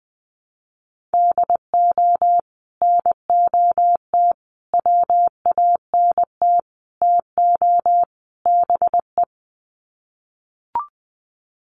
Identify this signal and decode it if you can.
Morse code, heard as a keyed tone, about 20 wpm, 700 Hz. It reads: DO NOT WANT TO BE
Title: Sets of 5 Words 20wpm